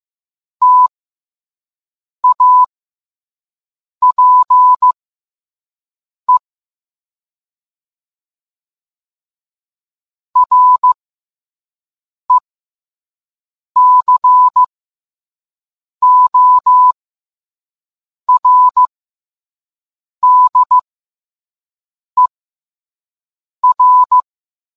Morse code, heard as a keyed tone, about 15 wpm, spaced as for 5 wpm, 1000 Hz. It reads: TAPE RECORDER